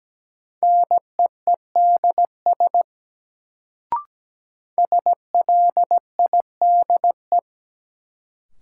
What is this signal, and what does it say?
Morse code, heard as a keyed tone, about 17 wpm, 700 Hz. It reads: NEEDS SLIDE
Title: Sentences from Top 1600 Words - New Word Drill Alt - 17wpm